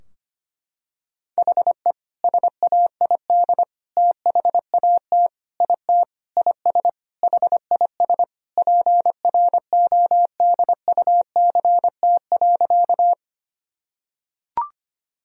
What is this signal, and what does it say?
Morse code, heard as a keyed tone, about 25 wpm, 700 Hz. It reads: HE SAID THAT IT IS HIS PRODUCT.
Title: Sentences from Top 1000 Words - Review Part 2 - 25wpm